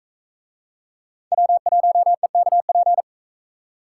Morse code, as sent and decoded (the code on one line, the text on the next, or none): .-- .---- . -.- .--.
W1EKP